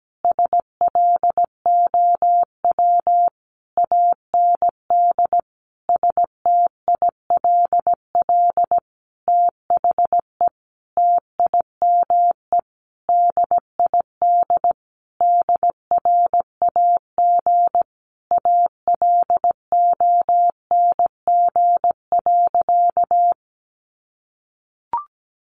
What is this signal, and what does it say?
Morse code, heard as a keyed tone, about 17 wpm, 700 Hz. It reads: SLOW AND STILL THE TIME DID DRAG ALONG.
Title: Sentences from Top 1500 Words 17wpm